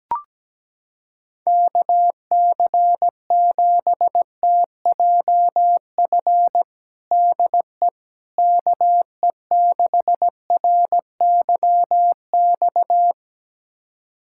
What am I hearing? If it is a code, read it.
Morse code, heard as a keyed tone, about 17 wpm, 700 Hz. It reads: KC7TJF DE KE6RYX
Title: USA Callsign DE Callsign - 6 Characters - 17wpm